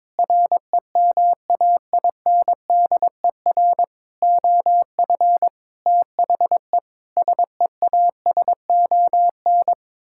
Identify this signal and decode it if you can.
Morse code, heard as a keyed tone, about 22 wpm, 700 Hz. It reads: REMAINDER OF THE SEASON
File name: practice / 22wpm / Sets of 6 Words - 22wpm